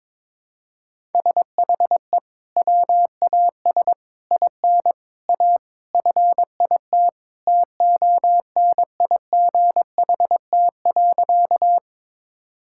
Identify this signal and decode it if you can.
Morse code, heard as a keyed tone, about 22 wpm, 700 Hz. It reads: SHE WAS IN A FIT TONIGHT.
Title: Sentences from Top 1000 Words - Review Part 5 - 22wpm